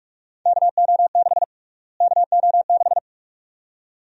Morse code, heard as a keyed tone, about 46 wpm, 700 Hz. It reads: KO6 KO6
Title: USA Call Sign Prefixes - Repeated 2x - 45wpm